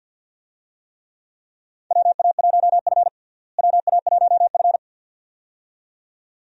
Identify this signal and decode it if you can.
Morse code, heard as a keyed tone, about 47 wpm, 700 Hz. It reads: WA1F WA1F
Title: USA Call Signs - 4 Characters - Repeated 2x - 50wpm